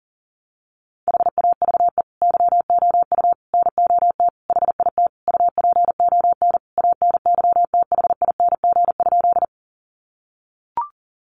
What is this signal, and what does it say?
Morse code, heard as a keyed tone, about 40 wpm, 700 Hz. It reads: HAVE YOU NOT HIT UPON ANYTHING?
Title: Sentences from Top 1200 Words 40wpm